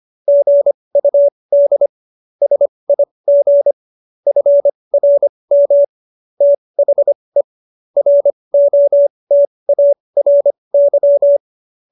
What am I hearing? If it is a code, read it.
Morse code, heard as a keyed tone, about 25 wpm, spaced as for 20 wpm, 570 Hz. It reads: GUD SIG FRM THE ROTARY